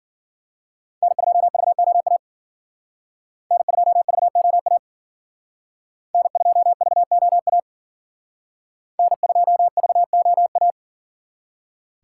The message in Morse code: N2VOA N2VOA N2VOA N2VOA

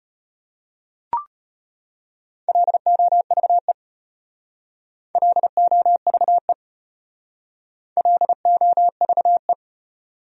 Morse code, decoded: LOVE LOVE LOVE

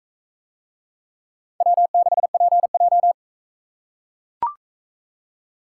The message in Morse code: W6PJ

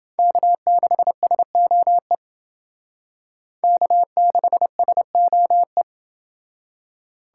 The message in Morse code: K6SOE K6SOE

